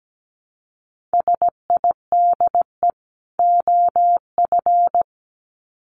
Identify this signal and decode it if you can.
Morse code, heard as a keyed tone, about 17 wpm, 700 Hz. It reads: SIDE OF
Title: Sets of 2 Words 17wpm